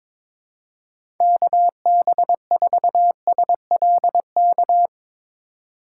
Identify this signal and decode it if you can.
Morse code, heard as a keyed tone, about 22 wpm, 700 Hz. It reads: KB4SLK
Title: USA Call Signs - 6 Characters - 2x Character Spacing - 22wpm